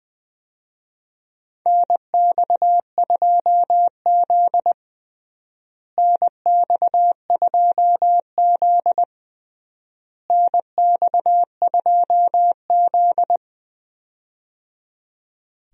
Morse code, heard as a keyed tone, about 20 wpm, 700 Hz. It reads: NX2Z NX2Z NX2Z
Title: USA Call Signs - 4 Characters - Repeated 3x - 20wpm